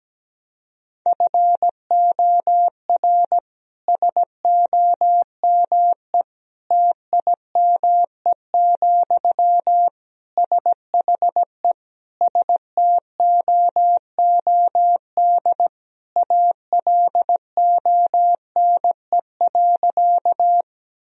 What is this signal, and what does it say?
Morse code, heard as a keyed tone, about 17 wpm, 700 Hz. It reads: FOR SOME TIME, SHE STOOD ALONE.
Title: Sentences from Top 1000 Words - Review Part 2 - 17wpm